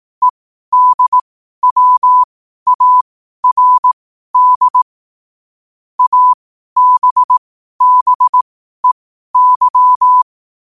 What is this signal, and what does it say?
Morse code, heard as a keyed tone, about 18 wpm, spaced as for 12 wpm, 1000 Hz. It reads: EDWARD ABBEY
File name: qotdfiles / 12wpm